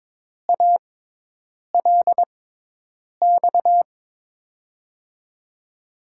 Morse code, heard as a keyed tone, about 22 wpm, spaced as for 8 wpm, 700 Hz. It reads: ALX
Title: USA Call Signs - 6 Characters - 6x Character Spacing - 22wpm